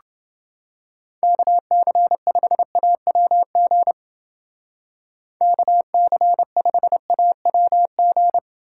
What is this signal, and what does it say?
Morse code, heard as a keyed tone, about 28 wpm, 700 Hz. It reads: KC5AWG KC5AWG